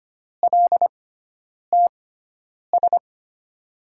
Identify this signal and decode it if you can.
Morse code, heard as a keyed tone, about 25 wpm, spaced as for 9 wpm, 700 Hz. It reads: LTS
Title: USA Call Signs - 6x Character Spacing - 25wpm